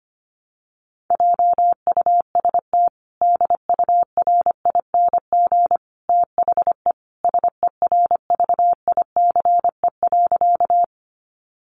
Morse code, heard as a keyed tone, about 25 wpm, 700 Hz. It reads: JUST DURING THE SERVICE.
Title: Sentences from Top 700 Words 25wpm